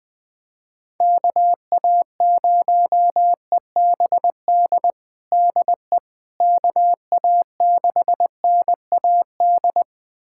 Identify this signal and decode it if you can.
Morse code, heard as a keyed tone, about 20 wpm, 700 Hz. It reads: KA0EBD DE KA6NAD
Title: USA Callsign DE Callsign - 6 Characters - 20wpm